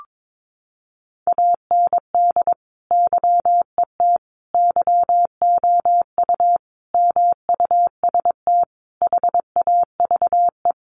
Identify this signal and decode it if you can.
Morse code, heard as a keyed tone, about 22 wpm, 700 Hz. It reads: AND YET YOU MUST HAVE
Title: Sentences from Top 1300 Words 22wpm